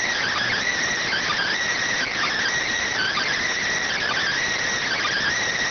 Emergency Alert System (SAME) tones (header bursts), heard as harsh buzzing bursts. It is heard from 1.0 to 5.4 s.